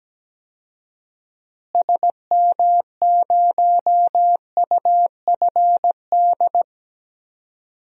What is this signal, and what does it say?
Morse code, heard as a keyed tone, about 17 wpm, 700 Hz. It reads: SM0UFD
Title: DXCC-100 Call Signs - 6 Characters - Repeated 3x - 17wpm